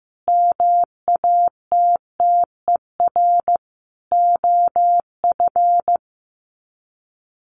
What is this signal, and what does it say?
Morse code, heard as a keyed tone, about 15 wpm, 700 Hz. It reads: MATTER OF
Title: Sets of 5 Words Encore2 15wpm